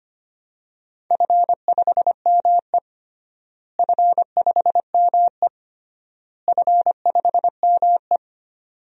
Morse code, heard as a keyed tone, about 25 wpm, 700 Hz. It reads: F5ME F5ME F5ME
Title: DXCC-100 Call Signs - 4 Characters - Repeated 3x - 25wpm